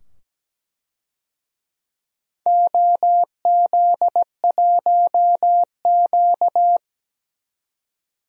Morse code, heard as a keyed tone, about 17 wpm, 700 Hz. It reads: OZ1Q
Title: DXCC-100 Call Signs - 4 Characters - 17wpm